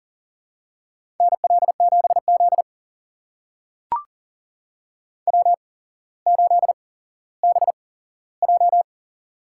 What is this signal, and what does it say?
Morse code, heard as a keyed tone, about 37 wpm, spaced as for 17 wpm, 700 Hz. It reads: NL7Z W8BJ
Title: USA Call Signs - 4 Characters - 8x Character Spacing - 40wpm